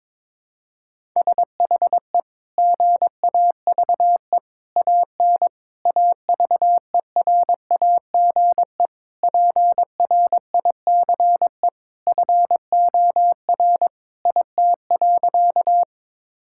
Morse code, heard as a keyed tone, about 22 wpm, 700 Hz. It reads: SHE GAVE AN AVERAGE PRICE FOR IT.
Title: Sentences from Top 1000 Words - Review Part 6 - 22wpm